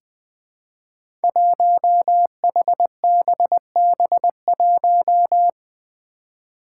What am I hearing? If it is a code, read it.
Morse code, heard as a keyed tone, about 20 wpm, 700 Hz. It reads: 1HBB1
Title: Five Challenging Letter-Number Combinations 20wpm